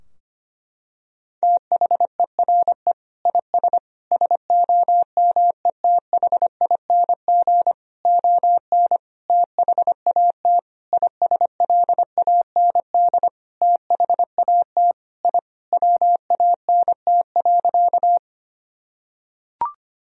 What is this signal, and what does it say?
Morse code, heard as a keyed tone, about 25 wpm, 700 Hz. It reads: THERE IS SOMETHING ON THAT ISLAND THAT I WANT.
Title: Sentences from Top 1000 Words - Review Part 3 - 25wpm